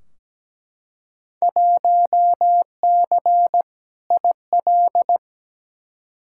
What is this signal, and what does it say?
Morse code, heard as a keyed tone, about 17 wpm, 700 Hz. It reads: 1C IL
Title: Field Day Exchanges - 17wpm